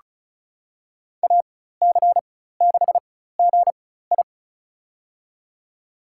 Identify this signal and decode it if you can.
Morse code, heard as a keyed tone, about 36 wpm, spaced as for 16 wpm, 700 Hz. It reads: AC6GI
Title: USA Call Signs - 4x Character Spacing - 35wpm